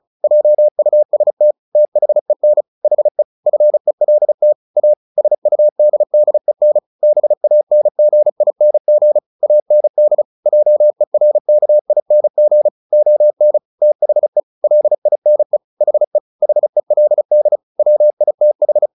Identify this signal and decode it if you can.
Morse code, heard as a keyed tone, about 34 wpm, 600 Hz. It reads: JUST THEN HE FELT A SUDDEN BANGING AND JERKING ON THE LINE HE HELD WITH
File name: the old man and the sea 2 35wpm